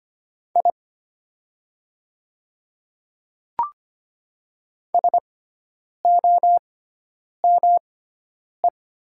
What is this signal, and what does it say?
Morse code, heard as a keyed tone, about 25 wpm, spaced as for 9 wpm, 700 Hz. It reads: I SOME
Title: 6x Character Spacing - Top 100 Words - 25wpm